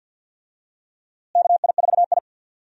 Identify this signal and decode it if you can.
Morse code, heard as a keyed tone, about 52 wpm, 700 Hz. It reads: KI4I
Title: USA Call Signs - 4 Characters - Repeated 2x - 50wpm